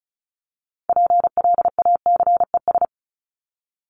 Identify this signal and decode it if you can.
Morse code, heard as a keyed tone, about 34 wpm, 700 Hz. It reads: PLACES
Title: ES:S Suffix - Top 100 Words with Base Words 35wpm